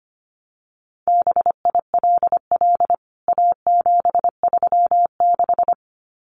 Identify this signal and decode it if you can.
Morse code, heard as a keyed tone, about 25 wpm, 700 Hz. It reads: BILL A736
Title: CWT Exchange 25wpm